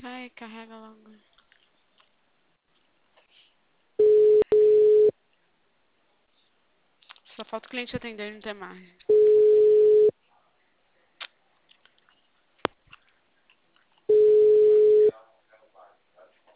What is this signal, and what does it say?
Ringback tone (Europe, 425 Hz), with speech.